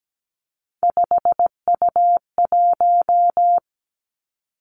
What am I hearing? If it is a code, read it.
Morse code, heard as a keyed tone, about 17 wpm, 700 Hz. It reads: 5U1
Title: Three Challenging Letter-Number Combinations 17wpm